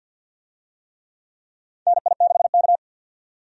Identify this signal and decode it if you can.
Morse code, heard as a keyed tone, about 52 wpm, 700 Hz. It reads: NI6K
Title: USA Call Signs - 4 Characters - 50wpm